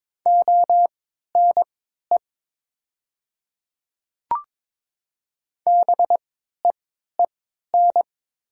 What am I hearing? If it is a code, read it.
Morse code, heard as a keyed tone, about 21 wpm, spaced as for 12 wpm, 700 Hz. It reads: ONE BEEN
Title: 3x Character Spacing - Top 100 Words - 22wpm